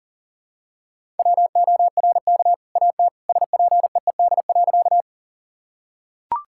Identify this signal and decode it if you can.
Morse code, heard as a keyed tone, about 40 wpm, 700 Hz. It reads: WORK AT SPEED.
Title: Sentences from Top 1000 Words - Review Part 5 - 40wpm